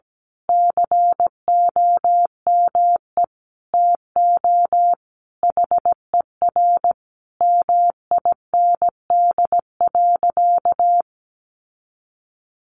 Morse code, heard as a keyed tone, about 17 wpm, 700 Hz. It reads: COME TO HER MIND.
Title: Sentences from Top 1300 Words 17wpm